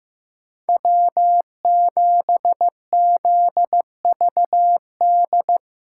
Morse code, heard as a keyed tone, about 18 wpm, 700 Hz. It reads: W7ZVD
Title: Speed Racing - USA 1x3 Call Signs 15wpm